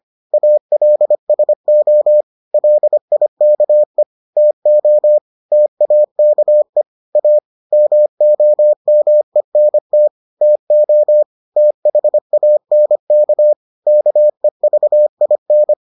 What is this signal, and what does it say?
Morse code, heard as a keyed tone, about 25 wpm, 600 Hz. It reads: ALSO LIKE TO TAKE A MOMENT TO THANK KEVIN